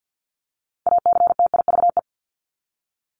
Morse code, heard as a keyed tone, about 50 wpm, 700 Hz. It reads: ACTIVE
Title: Top 751-1000 Words 50wpm